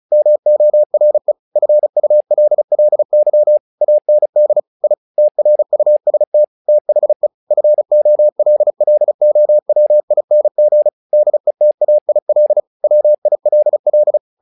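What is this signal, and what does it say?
Morse code, heard as a keyed tone, about 36 wpm, 600 Hz. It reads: MORE FULLY AND I TRUST THE FOLLOWING DETAIL WILL
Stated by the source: Morse practice text "Tower Trouble" sent at 35 WPM, from the SKCC - Straight Key Century Club learning center audio files.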